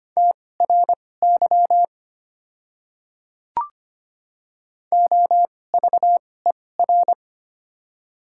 Morse code, heard as a keyed tone, about 25 wpm, spaced as for 18 wpm, 700 Hz. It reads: TRY OVER
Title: 2x Character Spacing - Top 300 Words - 25wpm